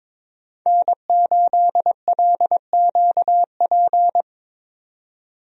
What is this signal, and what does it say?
Morse code, heard as a keyed tone, about 22 wpm, 700 Hz. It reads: N8LQP
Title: USA Call Signs - 5 Characters - 22wpm